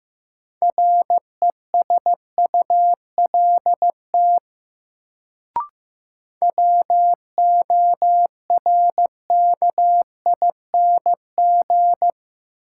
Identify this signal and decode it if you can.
Morse code, heard as a keyed tone, about 15 wpm, 700 Hz. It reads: RESULT WORKING
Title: Sentences from Top 700 Words - New Word Drill - 15wpm